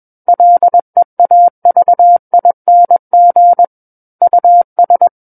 Morse code, heard as a keyed tone, about 21 wpm, 700 Hz. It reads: LEAVING US
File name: Book One / 12-Chapter 12 - 21 WPM